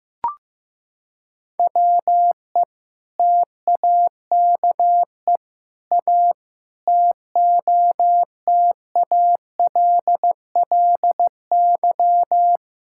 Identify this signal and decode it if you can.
Morse code, heard as a keyed tone, about 16 wpm, 700 Hz. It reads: WE TAKE A TOTALLY
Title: Sentences from Top 1600 Words 15wpm